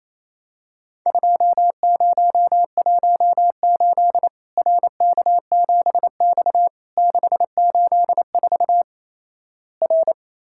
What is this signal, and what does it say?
Morse code, heard as a keyed tone, about 28 wpm, 700 Hz. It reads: 2018 RK7X 684 R
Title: Worked All Europe DX Contest - QTC Exchange - 28wpm